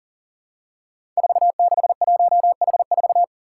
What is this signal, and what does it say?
Morse code, heard as a keyed tone, about 38 wpm, 700 Hz. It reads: 461H4
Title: Five Challenging Letter-Number Combinations 40wpm